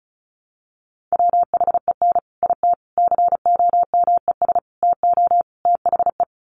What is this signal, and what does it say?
Morse code, heard as a keyed tone, about 34 wpm, 700 Hz. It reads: WHEN IT COMES TO THE
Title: Sets of 5 Words 35wpm